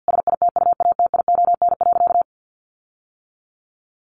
Morse code, heard as a keyed tone, about 50 wpm, 700 Hz. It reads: SITUATION.
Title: Sentences from Top 1300 Words 50wpm